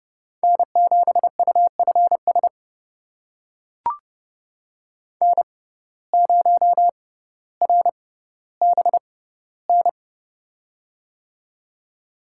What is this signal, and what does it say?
Morse code, heard as a keyed tone, about 29 wpm, spaced as for 15 wpm, 700 Hz. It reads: N7UFS N0RBN
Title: USA Call Signs - 5 Characters - 6x Character Spacing - 30wpm